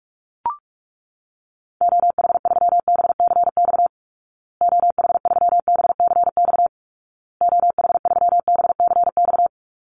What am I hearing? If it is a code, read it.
Morse code, heard as a keyed tone, about 44 wpm, 700 Hz. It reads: OH3BCX OH3BCX OH3BCX
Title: Call Signs Repeated 2x - 45wpm